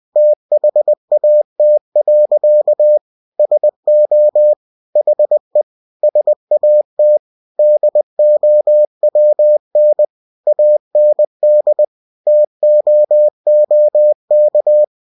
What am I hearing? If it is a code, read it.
Morse code, heard as a keyed tone, about 20 wpm, 600 Hz. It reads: THAT. SO HE SAT DOWN AND TOOK